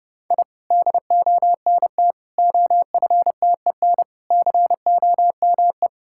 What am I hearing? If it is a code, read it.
Morse code, heard as a keyed tone, about 30 wpm, 700 Hz. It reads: I DONT OFTEN COME
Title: Sentences from Top 1000 Words - Review Part 6 - 30wpm